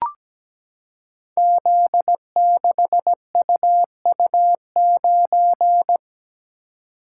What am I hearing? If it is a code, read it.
Morse code, heard as a keyed tone, about 17 wpm, 700 Hz. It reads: Z6UU9
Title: Code Groups - 5 Letter:Numbers - 17wpm